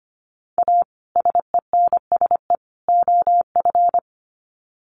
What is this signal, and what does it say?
Morse code, heard as a keyed tone, about 25 wpm, 700 Hz. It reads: A SENSE OF
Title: Sets of 3 Words 25wpm